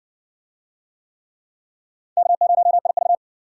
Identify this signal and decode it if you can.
Morse code, heard as a keyed tone, about 54 wpm, 700 Hz. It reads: K0IV